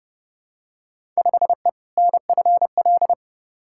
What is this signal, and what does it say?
Morse code, heard as a keyed tone, about 30 wpm, 700 Hz. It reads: 5E NFL
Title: Field Day Exchanges Encore - 30wpm